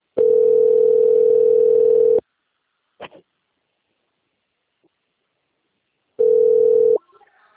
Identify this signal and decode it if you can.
Ringback tone (North America), 440 + 480 Hz.